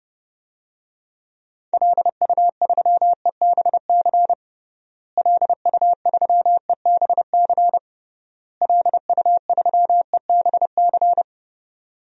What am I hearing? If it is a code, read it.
Morse code, heard as a keyed tone, about 32 wpm, 700 Hz. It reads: LU3EBC LU3EBC LU3EBC
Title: DXCC-100 Call Signs - 6 Characters - Repeated 3x - 30wpm